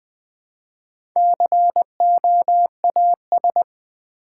Morse code, heard as a keyed tone, about 21 wpm, 700 Hz. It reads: COAS
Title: Build A Word - 1 Letter At A Time - 5-Letter Words - 20wpm